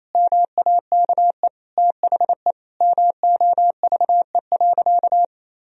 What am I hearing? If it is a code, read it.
Morse code, heard as a keyed tone, about 28 wpm, 700 Hz. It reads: MAKE THE MOVE.
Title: Sentences from Top 1000 Words - Review Part 1 - 28wpm